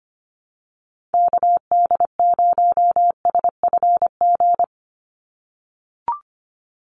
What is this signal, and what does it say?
Morse code, heard as a keyed tone, about 25 wpm, 700 Hz. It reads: KD0SFG
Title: USA Call Signs 25wpm